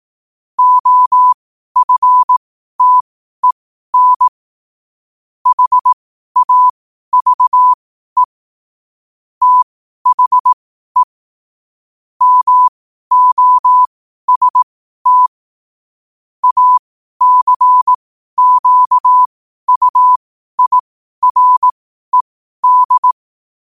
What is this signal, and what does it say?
Morse code, heard as a keyed tone, about 18 wpm, spaced as for 12 wpm, 1000 Hz. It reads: OFTEN HAVE THE MOST ACQUIRED
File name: qotdfiles / 12wpm